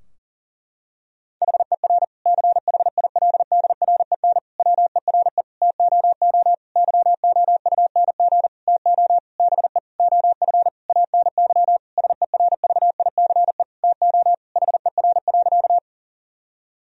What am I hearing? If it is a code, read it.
Morse code, heard as a keyed tone, about 37 wpm, 700 Hz. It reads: HER CHILDREN WERE TOO YOUNG TO BE OF ANY SERVICE TO HER.
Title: Sentences from Top 1000 Words - Review Part 4 - 40wpm